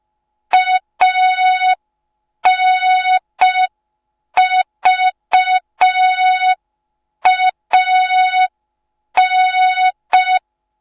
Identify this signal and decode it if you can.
Morse code, heard as a keyed tone, about 5 wpm, 3700 Hz. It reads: ANVAN